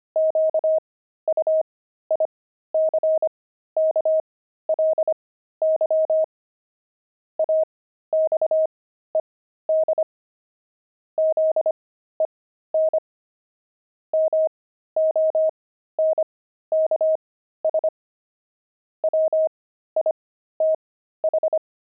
Morse code, heard as a keyed tone, about 25 wpm, spaced as for 13 wpm, 630 Hz. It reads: QUICKLY AXED ZEN MONKS WITH
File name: pangram-fs-13-cs-25-monks